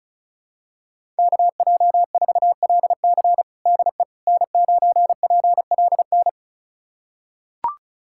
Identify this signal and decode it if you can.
Morse code, heard as a keyed tone, about 36 wpm, 700 Hz. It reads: KJ4LC DE N9PLN
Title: USA Callsign DE Callsign - 5 Characters - 35wpm